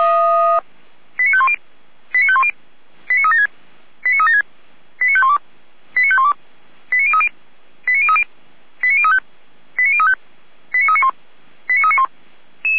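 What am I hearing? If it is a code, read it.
Five-tone selective call (ZVEI), heard as quick runs of five tones, tones of about 70 ms. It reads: 89510 89510 89376 89376 89512 89512 89030 89030 89035 89035 89391 89391
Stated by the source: ZVEI five-tone selective call received on an SDR.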